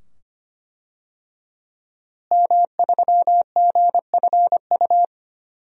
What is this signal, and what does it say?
Morse code, heard as a keyed tone, about 25 wpm, 700 Hz. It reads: M3GFU
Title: DXCC-100 Call Signs - 5 Characters - Repeated 2x - 25wpm